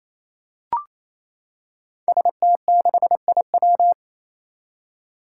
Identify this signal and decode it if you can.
Morse code, heard as a keyed tone, about 28 wpm, 700 Hz. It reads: ST6IW
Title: Five Challenging Letter-Number Combinations 28wpm